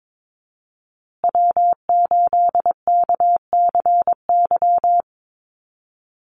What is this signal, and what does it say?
Morse code, heard as a keyed tone, about 22 wpm, 700 Hz. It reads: W8KCY
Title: USA Call Signs Repeated 3x 22wpm